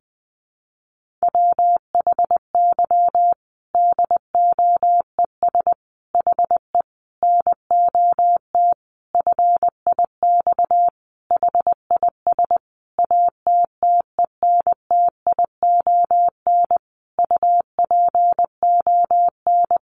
Morse code, heard as a keyed tone, about 20 wpm, 700 Hz. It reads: WHY DOES HE NOT FIX HIS ATTENTION UPON
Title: Sentences from Top 1400 Words 20wpm